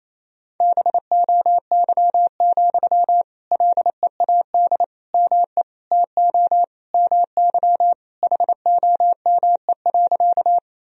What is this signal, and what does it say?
Morse code, heard as a keyed tone, about 27 wpm, 700 Hz. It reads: BOY, LEAD ME TO MY HOME.
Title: Sentences from Top 1000 Words - Review Part 2 - 28wpm